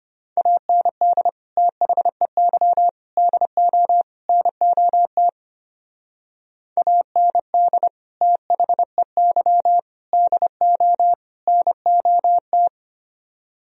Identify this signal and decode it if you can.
Morse code, heard as a keyed tone, about 32 wpm, 700 Hz. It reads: AND THEY DO NOT AND THEY DO NOT